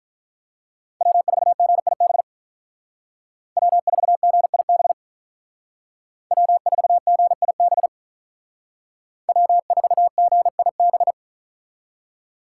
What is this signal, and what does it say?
Morse code, heard as a keyed tone, about 46 wpm, 700 Hz. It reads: W4GIB W4GIB W4GIB W4GIB